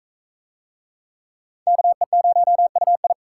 Morse code, heard as a keyed tone, about 36 wpm, 700 Hz. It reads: KE0UI